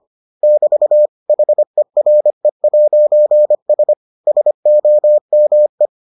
Morse code, heard as a keyed tone, about 25 wpm, 600 Hz. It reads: <BT> HERE'S SOME